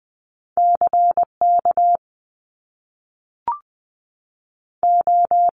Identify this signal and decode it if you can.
Morse code, heard as a keyed tone, about 20 wpm, 700 Hz. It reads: C K O
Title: Top Two Letter Combinations in Words 20wpm